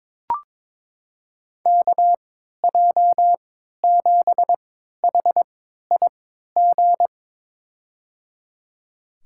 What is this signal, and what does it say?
Morse code, heard as a keyed tone, about 22 wpm, spaced as for 13 wpm, 700 Hz. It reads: KJ7HIG
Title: USA Call Signs - 6 Characters - 3x Character Spacing - 22wpm